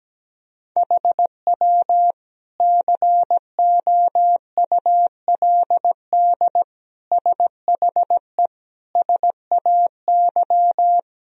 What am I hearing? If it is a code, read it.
Morse code, heard as a keyed tone, about 17 wpm, 700 Hz. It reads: HW COULD SHE SAY
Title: Sentences from Top 200 Words and Abbreviations 17wpm